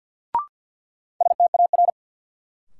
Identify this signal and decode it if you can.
Morse code, heard as a keyed tone, about 52 wpm, 700 Hz. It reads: STAR
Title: Sentences from Top 1000 Words - Review Part 2 - New Word Drill - 50wpm